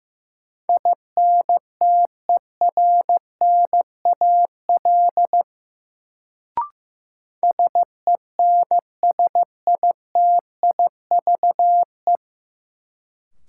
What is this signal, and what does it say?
Morse code, heard as a keyed tone, about 15 wpm, 700 Hz. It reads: INTERNAL SENSITIVE
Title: Sentences from Top 1400 Words - New Word Drill - 15wpm